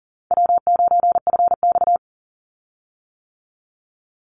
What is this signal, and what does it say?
Morse code, heard as a keyed tone, about 37 wpm, 700 Hz. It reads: W9FX
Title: Easy Call Signs - 40wpm